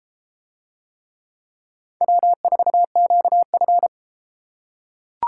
Morse code, heard as a keyed tone, about 31 wpm, 700 Hz. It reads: W4QF